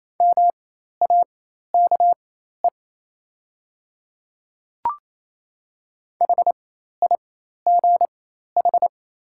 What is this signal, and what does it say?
Morse code, heard as a keyed tone, about 28 wpm, spaced as for 13 wpm, 700 Hz. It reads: MAKE HIGH
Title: 4x Character Spacing - Top 200 Words - 28wpm